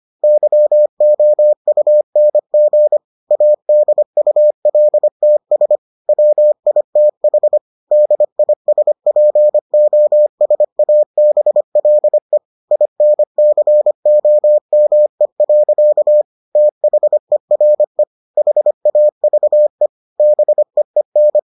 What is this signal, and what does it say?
Morse code, heard as a keyed tone, about 25 wpm, 600 Hz. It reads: YOUNG ADULTS WITH DISPOSABLE INCOME. THERE HAVE BEEN